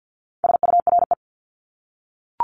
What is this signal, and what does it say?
Morse code, heard as a keyed tone, about 48 wpm, 700 Hz. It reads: SURE
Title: Top 300 Words 50wpm